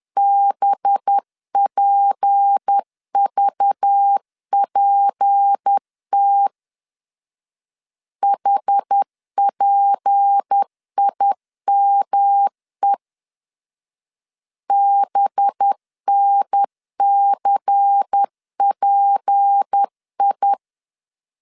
Morse code, heard as a keyed tone, about 10 wpm, 800 Hz. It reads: BPVPT HPIME BNCPI
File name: LEZIONI / 6 PKX0 / 6 P 10wpm 2432